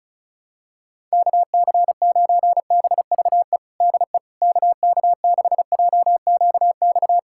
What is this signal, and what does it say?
Morse code, heard as a keyed tone, about 36 wpm, 700 Hz. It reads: KC9BVE DE KK6JQX
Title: USA Callsign DE Callsign - 6 Characters - 35wpm